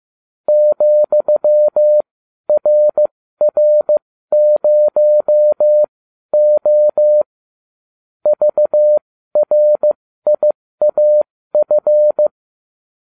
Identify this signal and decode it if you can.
Morse code, heard as a keyed tone, about 15 wpm, spaced as for 12 wpm, 600 Hz. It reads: ,RR0O VRIAF